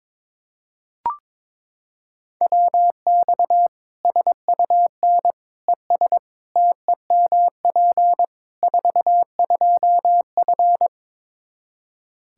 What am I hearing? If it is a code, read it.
Morse code, heard as a keyed tone, about 22 wpm, 700 Hz. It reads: WX SUN ES TEMP 42F
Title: K1USN SST QSO Practice 22wpm